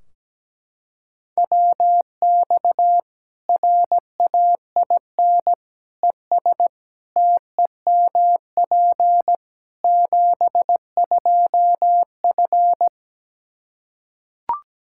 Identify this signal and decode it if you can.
Morse code, heard as a keyed tone, about 17 wpm, 700 Hz. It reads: WX RAIN ES TEMP 72F
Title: K1USN SST QSO Practice 17wpm